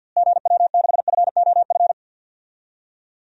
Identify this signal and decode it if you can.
Morse code, heard as a keyed tone, about 51 wpm, 700 Hz. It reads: GW6FOF
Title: DXCC-100 Call Signs - 6 Characters - Repeated 2x - 50wpm